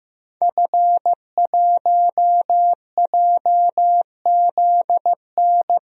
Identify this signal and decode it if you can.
Morse code, heard as a keyed tone, about 15 wpm, 700 Hz. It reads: F1JZN
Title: DXCC-100 Call Signs - 5 Characters - Repeated 2x - 15wpm